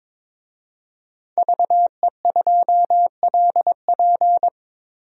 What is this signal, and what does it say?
Morse code, heard as a keyed tone, about 22 wpm, 700 Hz. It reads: VE2LP
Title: DXCC-100 Call Signs - 5 Characters - 22wpm